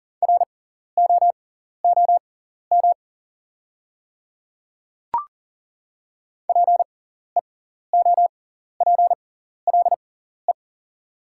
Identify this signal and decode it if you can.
Morse code, heard as a keyed tone, about 38 wpm, spaced as for 14 wpm, 690 Hz. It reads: ROOM PEOPLE
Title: 6x Character Spacing - Top 500 Words - 40wpm